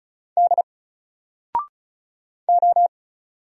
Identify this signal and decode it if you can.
Morse code, heard as a keyed tone, about 35 wpm, 700 Hz. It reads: D O
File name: Single-Letter-Number-35wpm